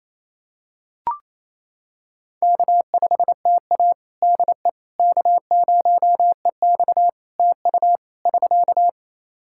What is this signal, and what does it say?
Morse code, heard as a keyed tone, about 28 wpm, 700 Hz. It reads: K5TA DE K0EX TU <SK>
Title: Practice QSO - Exchange 4 - Ending - 28wpm